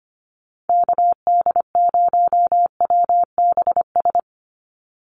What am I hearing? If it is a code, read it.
Morse code, heard as a keyed tone, about 25 wpm, 700 Hz. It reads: KD0WBS
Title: USA Call Signs 25wpm